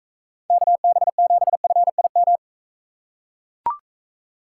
Morse code, heard as a keyed tone, about 43 wpm, 700 Hz. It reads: KB7FIM